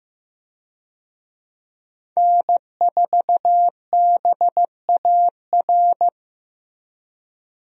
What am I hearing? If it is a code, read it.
Morse code, heard as a keyed tone, about 15 wpm, 700 Hz. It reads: N4BAR
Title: USA Call Signs - 5 Characters - 15wpm